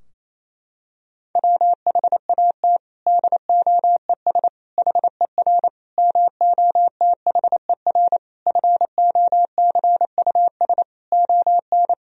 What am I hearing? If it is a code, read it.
Morse code, heard as a keyed tone, about 28 wpm, 700 Hz. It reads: WHAT DOES HER MOTHER FOCUS ON
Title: Sentences from Top 1000 Words - Review Part 4 - 28wpm